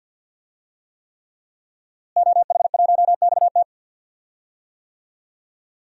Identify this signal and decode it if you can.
Morse code, heard as a keyed tone, about 51 wpm, 700 Hz. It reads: OH1XT
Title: DXCC-100 Call Signs - 5 Characters - Repeated 3x - 50wpm